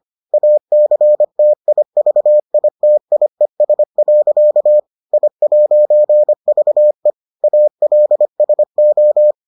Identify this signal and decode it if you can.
Morse code, heard as a keyed tone, about 25 wpm, 600 Hz. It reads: ACTIVITIES. I'VE ALSO